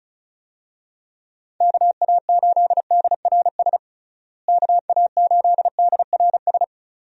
Morse code, heard as a keyed tone, about 34 wpm, 700 Hz. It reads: KA8DRS KA8DRS
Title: USA Call Signs - 6 Characters - Repeated 2x - 35wpm